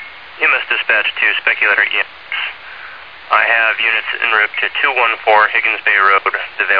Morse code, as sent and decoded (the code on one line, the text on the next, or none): none